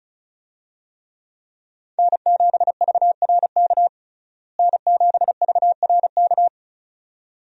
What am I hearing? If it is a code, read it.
Morse code, heard as a keyed tone, about 36 wpm, 700 Hz. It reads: N7VRK N7VRK
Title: USA Call Signs - 5 Characters - Repeated 2x - 35wpm